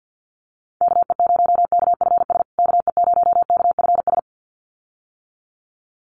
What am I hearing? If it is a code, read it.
Morse code, heard as a keyed tone, about 50 wpm, 700 Hz. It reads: KE0KFS KE0KFS
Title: USA Call Signs Repeated 2x 50wpm